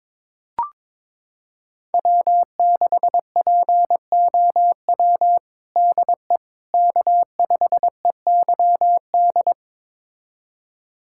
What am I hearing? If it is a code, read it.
Morse code, heard as a keyed tone, about 23 wpm, 700 Hz. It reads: W6POW DE K5EYD
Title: USA Callsign DE Callsign 22wpm